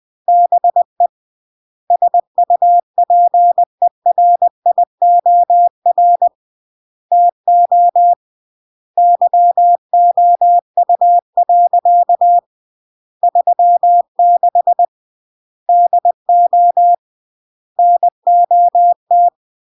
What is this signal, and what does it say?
Morse code, heard as a keyed tone, about 20 wpm, 700 Hz. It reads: BE SUPERIOR TO YOU. 36 DO NOT
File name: CW / jodorowsky / 82 commandments 20-20